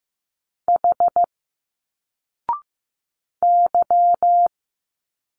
Morse code, heard as a keyed tone, about 15 wpm, 700 Hz. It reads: H Y
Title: Single Letters 15wpm